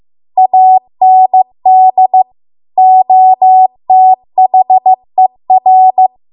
Morse code, heard as a keyed tone, about 15 wpm, 750 Hz. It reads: AND OTHER